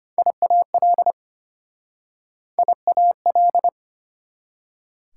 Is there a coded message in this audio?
Yes: Morse code from 0.2 to 3.7 s, heard as a keyed tone, about 33 wpm, 700 Hz. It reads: IAL IAL